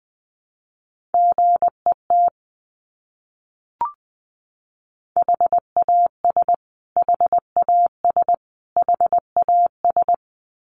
Morse code, heard as a keyed tone, about 20 wpm, 700 Hz. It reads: GET HAS HAS HAS
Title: Top 200 Words Repeated 20wpm